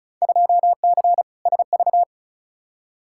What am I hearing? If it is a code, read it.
Morse code, heard as a keyed tone, about 36 wpm, 700 Hz. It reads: 2C SV
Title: Field Day Exchanges Encore - 35wpm